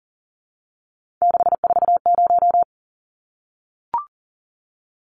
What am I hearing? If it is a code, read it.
Morse code, heard as a keyed tone, about 40 wpm, 700 Hz. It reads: 640
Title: Numbers - 3-Digit 40wpm